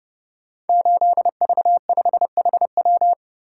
Morse code, heard as a keyed tone, about 28 wpm, 700 Hz. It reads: 8V5HW
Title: Code Groups - 5 Letter:Numbers - 30wpm